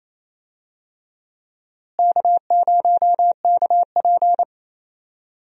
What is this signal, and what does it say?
Morse code, heard as a keyed tone, about 27 wpm, 700 Hz. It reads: K0KP
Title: USA Call Signs - 4 Characters - Repeated 3x - 28wpm